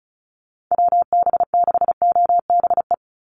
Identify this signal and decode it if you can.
Morse code, heard as a keyed tone, about 34 wpm, 700 Hz. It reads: WB6OBE
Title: USA Call Signs 35wpm